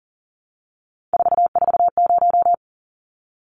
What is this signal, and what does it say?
Morse code, heard as a keyed tone, about 37 wpm, 700 Hz. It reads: 440
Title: Numbers - 3-Digit 40wpm